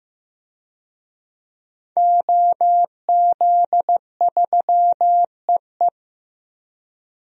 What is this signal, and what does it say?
Morse code, heard as a keyed tone, about 15 wpm, 700 Hz. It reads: OZ3EE
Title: DXCC-100 Call Signs - 5 Characters - 15wpm